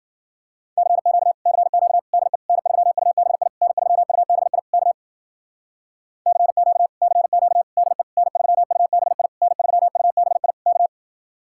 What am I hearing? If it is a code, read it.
Morse code, heard as a keyed tone, about 50 wpm, 700 Hz. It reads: CQ CQ DE N3UBI N3UBI K CQ CQ DE N3UBI N3UBI K
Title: Speed-Racing Common QSO Phrases - 40wpm